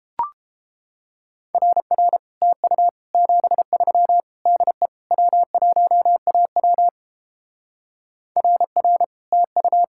Morse code, heard as a keyed tone, about 29 wpm, 700 Hz. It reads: RR TU 73 DE W1AW RR TU